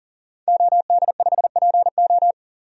Morse code, heard as a keyed tone, about 40 wpm, 700 Hz. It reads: OD5PO
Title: DXCC-100 Call Signs - 5 Characters - 40wpm